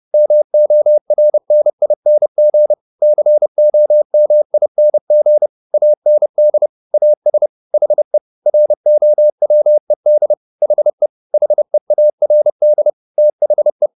Morse code, heard as a keyed tone, about 30 wpm, 600 Hz. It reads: MORNING COMING AND AS HE ROWED HE HEARD THE
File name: the old man and the sea / the old man and the sea 1 30wpm